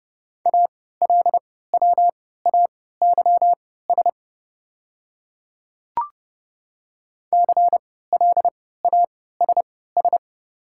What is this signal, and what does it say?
Morse code, heard as a keyed tone, about 32 wpm, spaced as for 17 wpm, 700 Hz. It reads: ALWAYS CLASS
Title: 3x Character Spacing - Top 500 Words - 30wpm